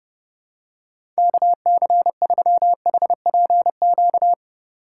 Morse code, heard as a keyed tone, about 30 wpm, 700 Hz. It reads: KC3HPQ
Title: USA Call Signs - 6 Characters - 4x Character Spacing - 30wpm